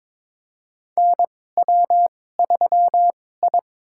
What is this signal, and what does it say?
Morse code, heard as a keyed tone, about 23 wpm, spaced as for 16 wpm, 700 Hz. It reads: NW3I
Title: USA Call Signs - 4 Characters - 2x Character Spacing - 22wpm